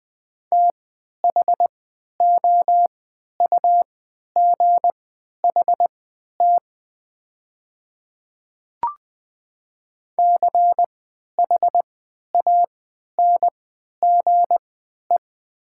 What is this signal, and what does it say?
Morse code, heard as a keyed tone, about 20 wpm, spaced as for 11 wpm, 700 Hz. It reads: THOUGHT CHANGE
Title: 3x Character Spacing - Top 300 Words - 20wpm